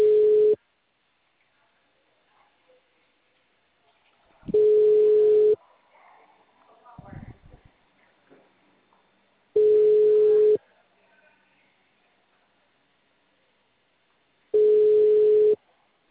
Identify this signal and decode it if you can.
Ringback tone (Europe, 425 Hz).
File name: out- -2001